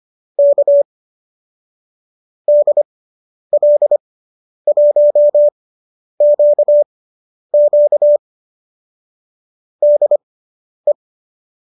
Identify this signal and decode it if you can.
Morse code, heard as a keyed tone, about 25 wpm, spaced as for 10 wpm, 590 Hz. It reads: K DL1QQ DE
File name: qso219 10